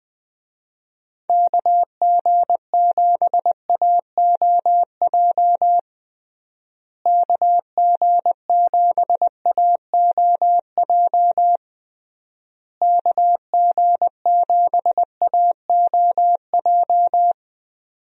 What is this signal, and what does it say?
Morse code, heard as a keyed tone, about 20 wpm, 700 Hz. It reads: KG7AOJ KG7AOJ KG7AOJ
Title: USA Call Signs - 6 Characters - Repeated 3x - 20wpm